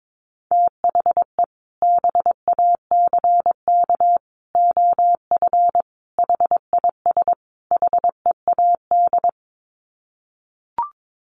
Morse code, heard as a keyed tone, about 22 wpm, 700 Hz. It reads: THE BACK OF HIS HEAD
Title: Sets of 5 Words Encore 22wpm